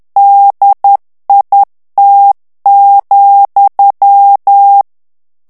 Morse code, heard as a keyed tone, about 10 wpm, 800 Hz. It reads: DIT,
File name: FRANCAIS TEXTE CLAIR 2 VITESSE 10 MOTS MINUTES